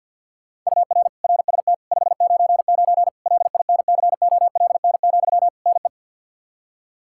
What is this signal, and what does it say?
Morse code, heard as a keyed tone, about 52 wpm, 700 Hz. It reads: UR RST 599 LINCOLN, NE
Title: Practice QSO - Exchange 1 - Beginning - 50wpm